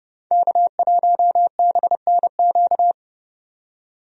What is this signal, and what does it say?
Morse code, heard as a keyed tone, about 30 wpm, 700 Hz. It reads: K1BNQ
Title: USA Call Signs - 5 Characters - 30wpm